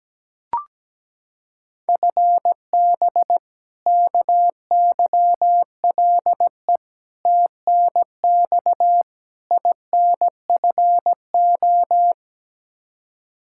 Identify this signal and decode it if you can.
Morse code, heard as a keyed tone, about 17 wpm, 700 Hz. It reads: FB KYLE TNX INFO
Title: Practice QSO - Exchange 3 - More Info - 17wpm